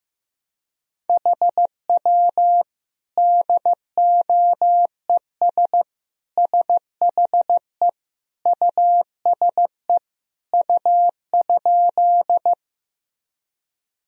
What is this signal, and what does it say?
Morse code, heard as a keyed tone, about 15 wpm, 700 Hz. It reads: HW DOES SHE USE U?
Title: Sentences from Top 200 Words and Abbreviations 15wpm